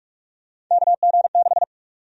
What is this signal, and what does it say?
Morse code, heard as a keyed tone, about 41 wpm, 700 Hz. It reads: KG6
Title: USA Call Sign Prefixes - Repeated 2x - 45wpm